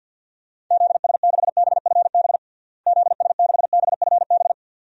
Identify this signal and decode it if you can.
Morse code, heard as a keyed tone, about 46 wpm, 700 Hz. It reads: ZS6BFB ZS6BFB
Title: DXCC-100 Call Signs - 6 Characters - Repeated 2x - 50wpm